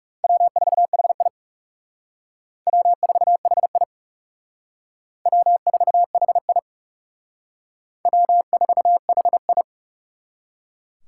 Morse code, heard as a keyed tone, about 38 wpm, 700 Hz. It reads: W4HI W4HI W4HI W4HI